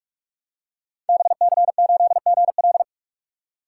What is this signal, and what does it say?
Morse code, heard as a keyed tone, about 44 wpm, 700 Hz. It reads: BC8GL
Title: Code Groups - 5 Letter:Numbers - 45wpm